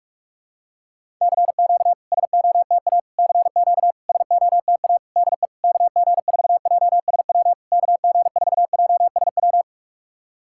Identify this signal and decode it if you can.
Morse code, heard as a keyed tone, about 46 wpm, 690 Hz. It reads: CQ SOTA CQ SOTA DE KG4JSW KG4JSW
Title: SOTA QSO Practice 45wpm